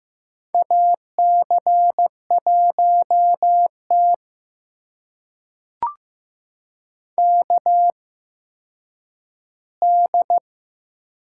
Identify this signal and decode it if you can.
Morse code, heard as a keyed tone, about 15 wpm, spaced as for 9 wpm, 700 Hz. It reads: AC1T KD
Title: USA Call Signs - 8x Character Spacing - 15wpm